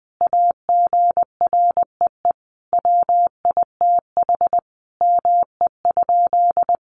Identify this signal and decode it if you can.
Morse code, heard as a keyed tone, about 20 wpm, 700 Hz. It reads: AGREE WITH ME?
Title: Sentences from Top 1100 Words 20wpm